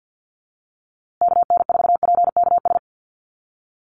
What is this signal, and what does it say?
Morse code, heard as a keyed tone, about 50 wpm, 700 Hz. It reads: KN4PKS
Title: USA Call Signs Repeated 2x 50wpm